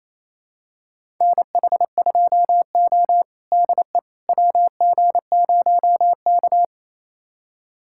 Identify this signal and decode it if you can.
Morse code, heard as a keyed tone, about 28 wpm, 700 Hz. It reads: NH2O DE WG0K
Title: USA Callsign DE Callsign - 4 Characters - 28wpm